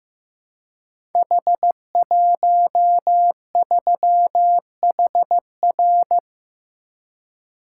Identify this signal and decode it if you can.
Morse code, heard as a keyed tone, about 15 wpm, 700 Hz. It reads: H13HR
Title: Code Groups - 5 Letter:Numbers - 15wpm